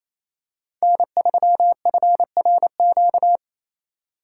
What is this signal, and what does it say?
Morse code, heard as a keyed tone, about 28 wpm, 700 Hz. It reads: N3FRQ
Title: USA Call Signs - 5 Characters - 4x Character Spacing - 28wpm